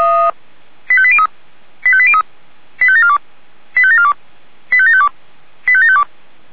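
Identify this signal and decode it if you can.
Five-tone selective call (ZVEI), heard as quick runs of five tones, tones of about 70 ms. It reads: 85803 85803 85742 85742 85752 85752
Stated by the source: ZVEI five-tone selective call received via SDR.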